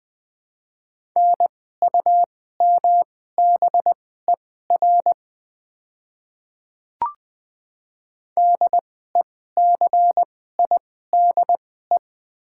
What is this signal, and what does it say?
Morse code, heard as a keyed tone, about 20 wpm, spaced as for 14 wpm, 690 Hz. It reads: NUMBER DECIDE
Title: 2x Character Spacing - Top 500 Words - 20wpm